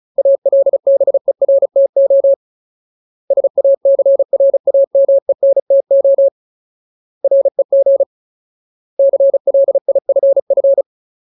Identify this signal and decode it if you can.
Morse code, heard as a keyed tone, about 35 wpm, 550 Hz. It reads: ALBERTO SACRAMENTO REG CLIFF
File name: Ultimate Word List / 35WPM Ultimate